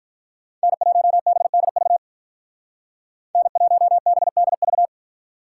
Morse code, heard as a keyed tone, about 46 wpm, 700 Hz. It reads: N1BDV N1BDV